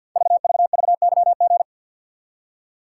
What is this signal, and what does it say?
Morse code, heard as a keyed tone, about 49 wpm, 700 Hz. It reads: VVVYG
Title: Code Groups - 5 Letters - 50wpm